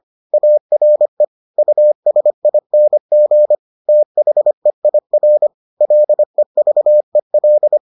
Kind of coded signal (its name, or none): Morse code